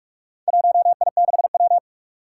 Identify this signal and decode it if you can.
Morse code, heard as a keyed tone, about 42 wpm, 700 Hz. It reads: 1I6W
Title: Four Challenging Letter-Number Combinations 45wpm